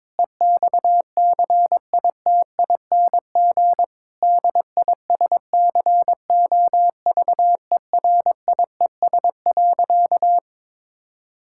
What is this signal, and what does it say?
Morse code, heard as a keyed tone, about 22 wpm, 700 Hz. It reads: EXCITING DISCOVERIES.
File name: Difficult Sentences with Hints - 22wpm